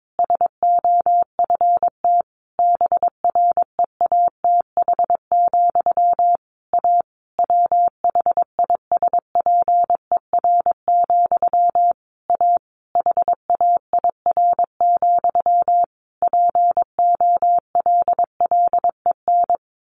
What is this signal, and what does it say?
Morse code, heard as a keyed tone, about 22 wpm, 700 Hz. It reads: SOFT BREATH, A WHISPER, A HAIR, POLLEN
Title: A Sound Of Thunder 22wpm